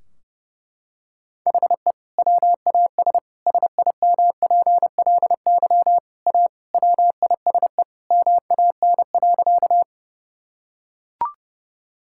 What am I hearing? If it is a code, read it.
Morse code, heard as a keyed tone, about 32 wpm, 700 Hz. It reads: HE WAS SIMPLY A WISE MAN.
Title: Sentences from Top 1600 Words 30wpm